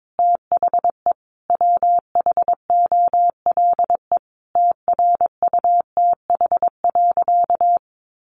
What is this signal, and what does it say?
Morse code, heard as a keyed tone, about 22 wpm, 700 Hz. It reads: THE WHOLE TRUTH.
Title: Sentences from Top 800 Words 22wpm